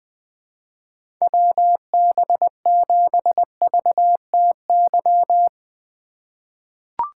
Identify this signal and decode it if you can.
Morse code, heard as a keyed tone, about 20 wpm, 700 Hz. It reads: WB7VTY
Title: 2023 SOTA Chaser Call Signs - 20wpm